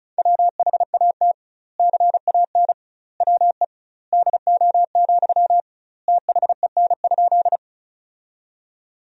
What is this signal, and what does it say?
Morse code, heard as a keyed tone, about 36 wpm, 700 Hz. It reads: WHAT CAN WE DO, THEN?
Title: 2x Word-Spacing - Sentences from Top 100 Words - 35wpm